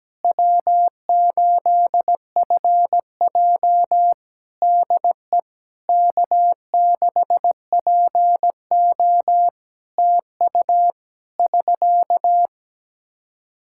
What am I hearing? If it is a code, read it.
Morse code, heard as a keyed tone, about 17 wpm, 700 Hz. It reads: W8FJ DE K6PO TU <SK>
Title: Practice QSO - Exchange 4 - Ending - 17wpm